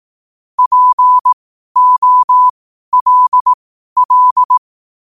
Morse code, heard as a keyed tone, about 18 wpm, spaced as for 12 wpm, 1000 Hz. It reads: POLL
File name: qotdfiles / 12wpm